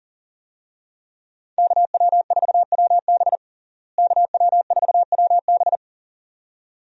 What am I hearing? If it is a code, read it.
Morse code, heard as a keyed tone, about 37 wpm, 700 Hz. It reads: KW4WB KW4WB
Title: USA Call Signs - 5 Characters - Repeated 2x - 40wpm